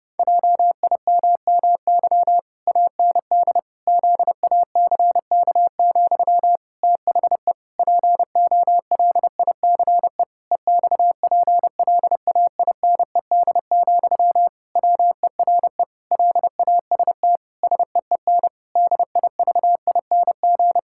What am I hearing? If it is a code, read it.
Morse code, heard as a keyed tone, about 30 wpm, 700 Hz. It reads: JIMMY AND ZACK, THE POLICE EXPLAINED, WERE LAST SEEN DIVING
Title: Pangram Challenge v2 - 30wpm